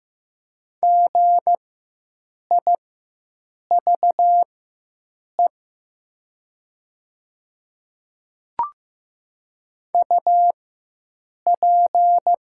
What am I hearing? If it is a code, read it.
Morse code, heard as a keyed tone, about 15 wpm, spaced as for 7 wpm, 700 Hz. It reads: GIVE UP
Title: 4x Character Spacing - Top 200 Words - 15wpm